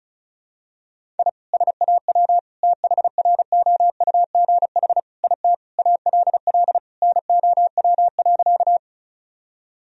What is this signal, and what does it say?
Morse code, heard as a keyed tone, about 36 wpm, 700 Hz. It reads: I SAW THROUGH IT ALL NOW.
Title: Sentences from Top 1000 Words - Review Part 1 - 35wpm